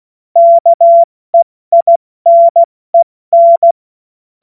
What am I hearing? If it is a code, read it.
Morse code, heard as a keyed tone, about 16 wpm, 680 Hz. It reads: KEINEN
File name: Morsekurs / Witze / Witz 25. Mai 2021, 70 F80